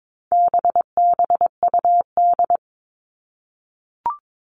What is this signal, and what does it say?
Morse code, heard as a keyed tone, about 22 wpm, 700 Hz. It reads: BBUD
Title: All Cut Numbers - 4-Digit 22wpm